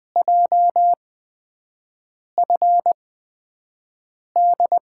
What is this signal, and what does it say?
Morse code, heard as a keyed tone, about 20 wpm, spaced as for 5 wpm, 700 Hz. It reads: JFD
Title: USA Call Signs - 8x Character Spacing - 20wpm